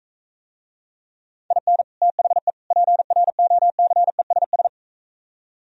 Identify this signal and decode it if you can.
Morse code, heard as a keyed tone, about 43 wpm, 700 Hz. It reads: IN THE PROCESS